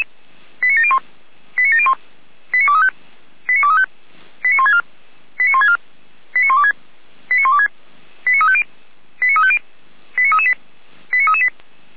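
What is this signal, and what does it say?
Five-tone selective call (ZVEI), heard as quick runs of five tones, tones of about 70 ms. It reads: 89701 89701 89235 89235 89164 89164 89126 89126 89350 89350 89338 89338
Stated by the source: Five-tone ZVEI selective call received on an SDR.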